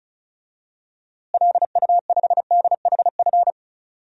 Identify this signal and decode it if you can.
Morse code, heard as a keyed tone, about 34 wpm, 700 Hz. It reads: LU5DHF
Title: DXCC-100 Call Signs - 6 Characters - Repeated 3x - 35wpm